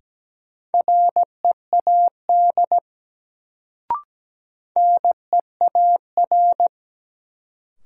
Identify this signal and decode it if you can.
Morse code, heard as a keyed tone, about 17 wpm, 700 Hz. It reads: READ NEAR
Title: Sentences from Top 200 Words - New Word Drill - 17wpm